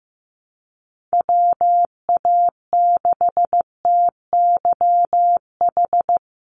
Morse code, heard as a keyed tone, about 15 wpm, 700 Hz. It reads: WA6TYH
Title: USA Call Signs 15wpm